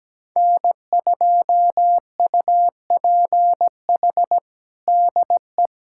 Morse code, heard as a keyed tone, about 17 wpm, 700 Hz. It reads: N2UPH DE
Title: USA Callsign DE Callsign - 5 Characters - 17wpm